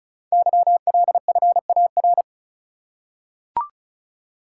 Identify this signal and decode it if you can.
Morse code, heard as a keyed tone, about 36 wpm, 700 Hz. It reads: YLFAR
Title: Code Groups - 5 Letters - 35wpm